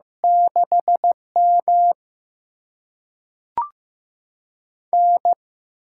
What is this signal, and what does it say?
Morse code, heard as a keyed tone, about 15 wpm, 700 Hz. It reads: 6M N
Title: USA Call Signs - 4 Characters - 6x Character Spacing - 15wpm